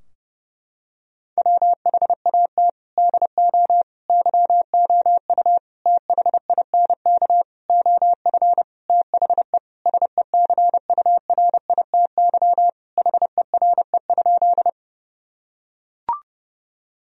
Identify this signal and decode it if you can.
Morse code, heard as a keyed tone, about 30 wpm, 700 Hz. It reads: WHAT DO YOU THINK OF THE SECURITY HERE?
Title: Sentences from Top 1000 Words - Review Part 4 - 30wpm